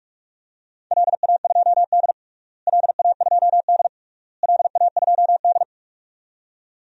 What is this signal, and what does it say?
Morse code, heard as a keyed tone, about 41 wpm, 700 Hz. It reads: LA2D LA2D LA2D
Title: DXCC-100 Call Signs - 4 Characters - Repeated 3x - 45wpm